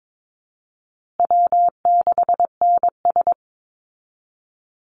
Morse code, heard as a keyed tone, about 22 wpm, 700 Hz. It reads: W6NS
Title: Easy Call Signs Repeated 2x - 22wpm